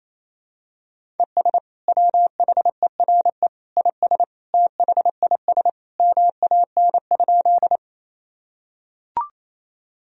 Morse code, heard as a keyed tone, about 27 wpm, 700 Hz. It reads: ES WHERE IS THIS MAN?
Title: Sentences from Top 200 Words and Abbreviations 28wpm